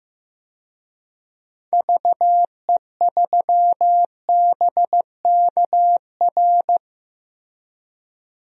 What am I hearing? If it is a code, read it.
Morse code, heard as a keyed tone, about 15 wpm, 700 Hz. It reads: VE3BKR
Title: DXCC-100 Call Signs - 6 Characters - 15wpm